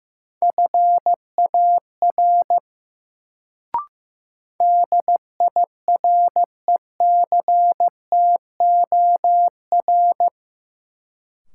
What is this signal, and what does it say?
Morse code, heard as a keyed tone, about 15 wpm, 700 Hz. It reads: FAR DIRECTOR
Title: Sets of 3 Words - New Word Drill - 15wpm